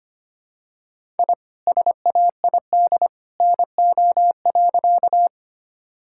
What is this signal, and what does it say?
Morse code, heard as a keyed tone, about 25 wpm, 700 Hz. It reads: I SAID NO.
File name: practice / 25wpm / Sentences with 2-Word Phrases - 25wpm